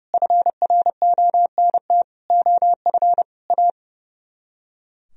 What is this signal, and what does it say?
Morse code, heard as a keyed tone, about 39 wpm, 700 Hz. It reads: FRONT OF A